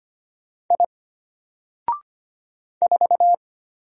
Morse code, heard as a keyed tone, about 25 wpm, 700 Hz. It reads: I 4
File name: Single-Letter-Number-25wpm